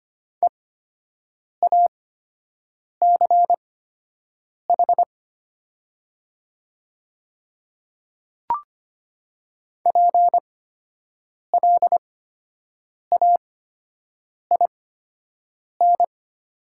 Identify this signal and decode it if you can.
Morse code, heard as a keyed tone, about 25 wpm, spaced as for 7 wpm, 690 Hz. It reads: EACH PLAIN
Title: 8x Character Spacing - Top 500 Words - 25wpm